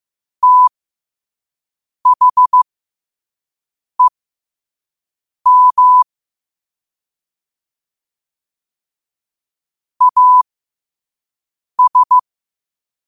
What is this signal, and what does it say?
Morse code, heard as a keyed tone, about 15 wpm, spaced as for 5 wpm, 1000 Hz. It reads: THEM AS
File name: qotdfiles / 5wpm